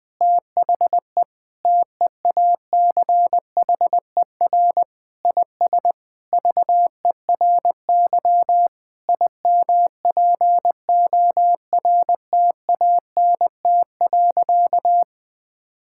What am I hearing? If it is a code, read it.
Morse code, heard as a keyed tone, about 21 wpm, 700 Hz. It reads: THE TEACHER IS VERY IMPORTANT.
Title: Sentences with 4-Word Phrases - 20wpm